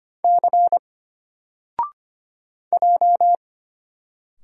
Morse code, heard as a keyed tone, about 25 wpm, 700 Hz. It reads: C J
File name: Single-Letter-Number-25wpm